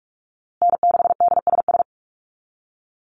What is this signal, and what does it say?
Morse code, heard as a keyed tone, about 44 wpm, 700 Hz. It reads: N6DSS
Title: USA Call Signs Repeated 2x 45wpm